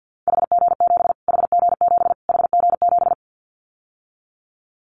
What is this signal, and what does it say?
Morse code, heard as a keyed tone, about 50 wpm, 700 Hz. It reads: HG7 HG7 HG7
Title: Callsign Prefixes for Top 25 DXCC - Repeated 3x - 50wpm